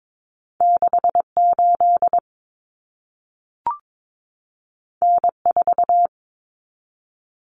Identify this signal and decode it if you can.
Morse code, heard as a keyed tone, about 22 wpm, 700 Hz. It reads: 68 N4
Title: Common Cut Numbers - 2-Digit 22wpm